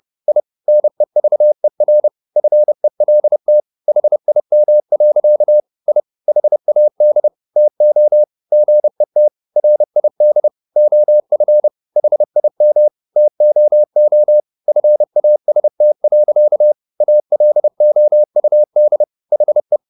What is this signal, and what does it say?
Morse code, heard as a keyed tone, about 30 wpm, 600 Hz. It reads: I NEVER FELT HIM. I HAD TO GET RID OF HIM TOO FAST. ALOUD HE
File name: the old man and the sea / the old man and the sea 1 30wpm